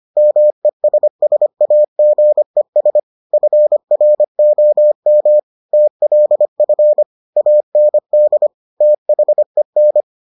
Morse code, heard as a keyed tone, about 25 wpm, 600 Hz. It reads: MESSAGES FROM TLF AND THEN